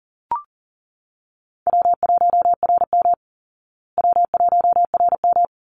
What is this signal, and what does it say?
Morse code, heard as a keyed tone, about 40 wpm, 700 Hz. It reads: W1RM W1RM
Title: Easy Call Signs Repeated 2x - 40wpm